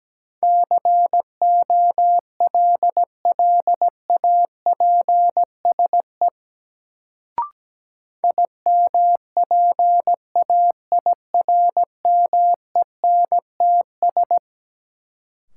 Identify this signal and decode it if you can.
Morse code, heard as a keyed tone, about 17 wpm, 700 Hz. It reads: COLLAPSE IMPAIRMENTS
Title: Sets of 5 Words Encore - New Word Drill - 17wpm